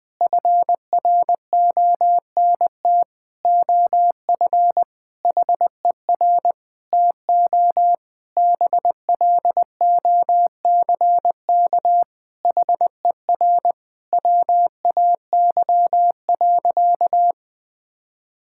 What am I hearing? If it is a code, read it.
Morse code, heard as a keyed tone, about 19 wpm, 700 Hz. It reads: FRONT OF HER TO BLOCK HER WAY.
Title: Sentences from Top 1600 Words 20wpm